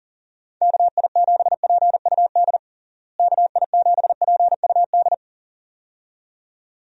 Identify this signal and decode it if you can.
Morse code, heard as a keyed tone, about 40 wpm, 700 Hz. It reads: KI7PUD KI7PUD
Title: USA Call Signs - 6 Characters - Repeated 2x - 40wpm